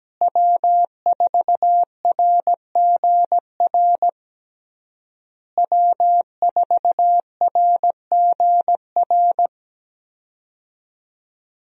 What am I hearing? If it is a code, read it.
Morse code, heard as a keyed tone, about 17 wpm, 700 Hz. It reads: W4RGR W4RGR
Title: USA Call Signs - 5 Characters - Repeated 3x - 17wpm